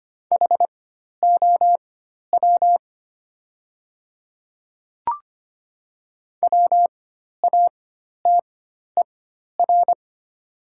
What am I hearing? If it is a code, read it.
Morse code, heard as a keyed tone, about 25 wpm, spaced as for 12 wpm, 690 Hz. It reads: HOW WATER
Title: 4x Character Spacing - Top 100 Words - 25wpm